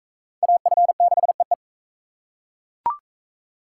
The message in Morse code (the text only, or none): AF6EE